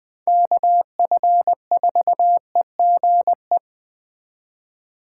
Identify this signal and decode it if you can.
Morse code, heard as a keyed tone, about 20 wpm, 700 Hz. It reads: KF4EGE
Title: USA Call Signs - 8x Character Spacing - 20wpm